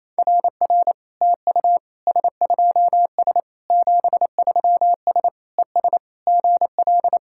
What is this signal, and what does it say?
Morse code, heard as a keyed tone, about 29 wpm, 700 Hz. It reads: RR TU S2S 73S ES GL
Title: SOTA QSO Practice 28wpm